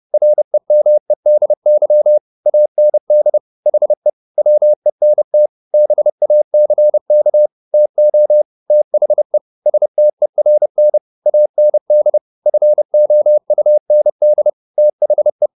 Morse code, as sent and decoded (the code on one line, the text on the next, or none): .-. -- . -.. -.-- / .- -. -.. / .... / .-- . -. - / -... .- -.-. -.- / - --- / - .... / ... - . .-. -. / .- -. -.. / ..-. --- ..- -. -.. / - .... .
REMEDY AND HE WENT BACK TO THE STERN AND FOUND THE